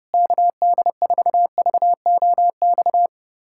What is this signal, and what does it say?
Morse code, heard as a keyed tone, about 32 wpm, 700 Hz. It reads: KD4VOX
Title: USA Callsign DE Callsign - 6 Characters - 30wpm